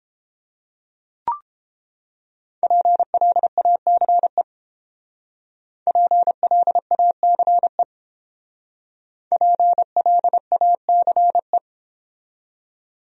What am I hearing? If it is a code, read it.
Morse code, heard as a keyed tone, about 30 wpm, 700 Hz. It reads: PLACE PLACE PLACE